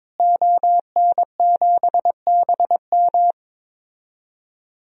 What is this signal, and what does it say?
Morse code, heard as a keyed tone, about 22 wpm, 700 Hz. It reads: ON7BM
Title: DXCC-100 Call Signs - 5 Characters - Repeated 3x - 22wpm